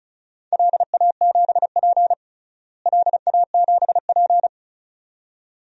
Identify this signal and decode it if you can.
Morse code, heard as a keyed tone, about 34 wpm, 700 Hz. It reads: LA7P LA7P
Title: DXCC-100 Call Signs - 4 Characters - Repeated 2x - 35wpm